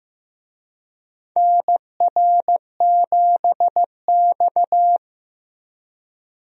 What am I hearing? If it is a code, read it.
Morse code, heard as a keyed tone, about 15 wpm, 700 Hz. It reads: NR7X
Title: USA Call Signs - 4 Characters - Repeated 3x - 15wpm